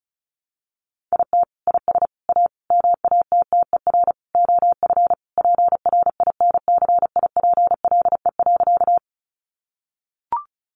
Morse code, heard as a keyed tone, about 34 wpm, 700 Hz. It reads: IT IS A MATTER OF PRINCIPLE.
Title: Sentences from Top 1400 Words 35wpm